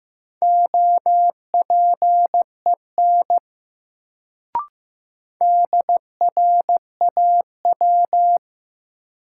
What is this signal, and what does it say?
Morse code, heard as a keyed tone, about 15 wpm, 700 Hz. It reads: OPEN DRAW
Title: Sentences from Top 300 Words - New Word Drill - 15wpm